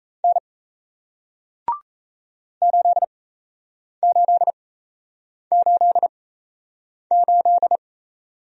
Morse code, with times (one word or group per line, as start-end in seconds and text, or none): 0.2-0.4 N
2.6-7.8 8888